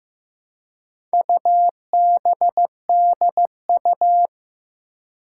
Morse code, heard as a keyed tone, about 15 wpm, 700 Hz. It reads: UBDU
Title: Four Challenging Letter-Number Combinations 15wpm